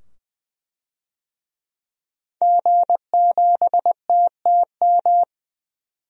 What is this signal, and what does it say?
Morse code, heard as a keyed tone, about 20 wpm, 700 Hz. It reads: G7TTM
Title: DXCC-100 Call Signs - 5 Characters - 20wpm